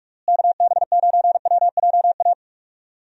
Morse code, heard as a keyed tone, about 46 wpm, 700 Hz. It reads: KB9WJA
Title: USA Call Signs - 6 Characters - Repeated 3x - 45wpm